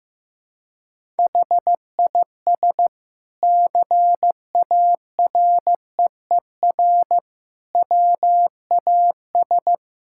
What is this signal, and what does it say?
Morse code, heard as a keyed tone, about 15 wpm, 700 Hz. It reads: HIS CAREER WAS
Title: Sentences from Top 1000 Words - Review Part 6 - 15wpm